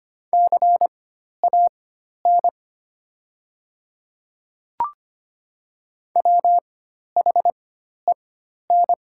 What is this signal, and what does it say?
Morse code, heard as a keyed tone, about 25 wpm, spaced as for 12 wpm, 700 Hz. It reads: CAN WHEN
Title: 4x Character Spacing - Top 300 Words - 25wpm